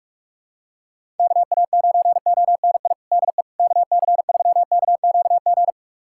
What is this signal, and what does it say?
Morse code, heard as a keyed tone, about 46 wpm, 700 Hz. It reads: KA9ONI DE KC3KQG
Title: USA Callsign DE Callsign - 6 Characters - 45wpm